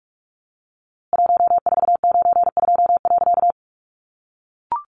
Morse code, heard as a keyed tone, about 44 wpm, 700 Hz. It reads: 1492.
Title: A Sound Of Thunder 45wpm